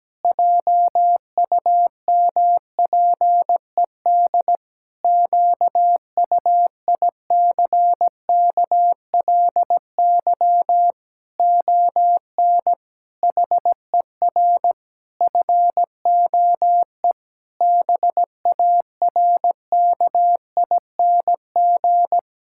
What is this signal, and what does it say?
Morse code, heard as a keyed tone, about 17 wpm, 700 Hz. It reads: JUMPED QUICKLY ON HER FOE BARKING
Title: Pangram Challenge v2 - 17wpm